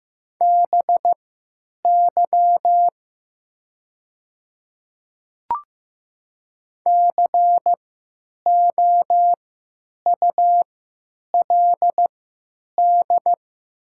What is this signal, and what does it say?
Morse code, heard as a keyed tone, about 15 wpm, spaced as for 9 wpm, 700 Hz. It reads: BY COULD
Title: 3x Character Spacing - Top 100 Words - 15wpm